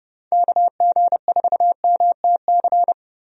Morse code, heard as a keyed tone, about 30 wpm, 700 Hz. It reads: KG4MTC
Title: USA Call Signs - 6 Characters - Repeated 3x - 30wpm